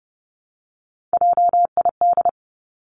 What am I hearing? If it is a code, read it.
Morse code, heard as a keyed tone, about 32 wpm, 700 Hz. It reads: JID
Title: Three Challenging Letter-Number Combinations 30wpm